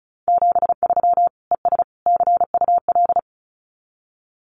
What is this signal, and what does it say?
Morse code, heard as a keyed tone, about 34 wpm, 700 Hz. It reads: 73 ES CUL
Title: Common QSO Phrases Encore 35wpm